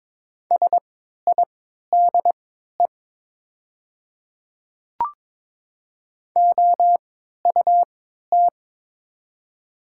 Morse code, heard as a keyed tone, about 22 wpm, spaced as for 12 wpm, 700 Hz. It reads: SIDE OUT
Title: 3x Character Spacing - Top 100 Words - 22wpm